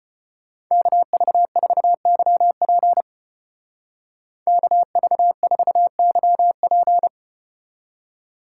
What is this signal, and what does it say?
Morse code, heard as a keyed tone, about 34 wpm, 700 Hz. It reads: KV4YP KV4YP